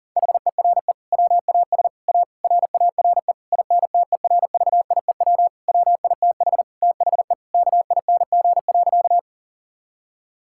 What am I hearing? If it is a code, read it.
Morse code, heard as a keyed tone, about 40 wpm, 700 Hz. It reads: HERE WAS A RARE INTERVIEW WITH THE KING.
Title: Sentences from Top 1600 Words 40wpm